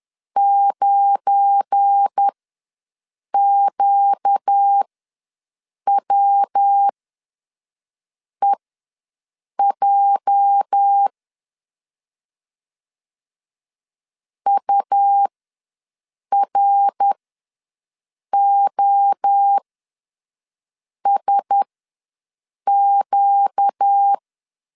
Morse code, heard as a keyed tone, about 10 wpm, spaced as for 5 wpm, 800 Hz. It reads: 9QWEJ UROSQ
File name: LEZIONI / 8 YJ45 / 8 YJ ws 10wpm 4043